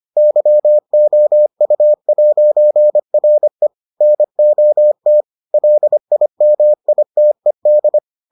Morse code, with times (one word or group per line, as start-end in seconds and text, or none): 0.2-3.7 YOU'RE
4.0-5.2 NOT
5.5-8.0 LIMITED